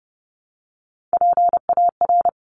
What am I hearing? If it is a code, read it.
Morse code, heard as a keyed tone, about 30 wpm, 700 Hz. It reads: PAR
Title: Top Three Letter Combinations in Words 30wpm